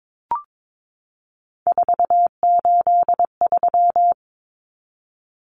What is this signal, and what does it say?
Morse code, heard as a keyed tone, about 22 wpm, 700 Hz. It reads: 483
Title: Numbers - 3-Digit 22wpm